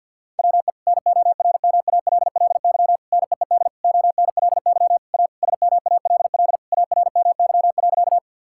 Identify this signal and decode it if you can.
Morse code, heard as a keyed tone, about 52 wpm, 700 Hz. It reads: WE NORMALLY NEED ONLY A SMALL ARMY.
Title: Sentences from Top 1000 Words - Review Part 6 - 50wpm